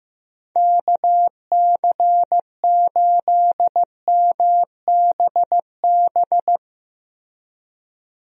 Morse code, heard as a keyed tone, about 16 wpm, 700 Hz. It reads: KC8MBB
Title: USA Call Signs - 6 Characters - Repeated 2x - 15wpm